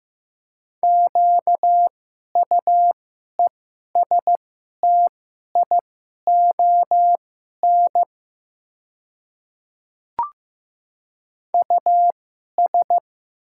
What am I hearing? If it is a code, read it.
Morse code, heard as a keyed tone, about 15 wpm, spaced as for 11 wpm, 700 Hz. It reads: QUESTION US
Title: 2x Character Spacing - Top 500 Words - 15wpm